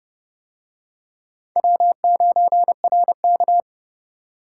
Morse code, heard as a keyed tone, about 29 wpm, 700 Hz. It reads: W9RK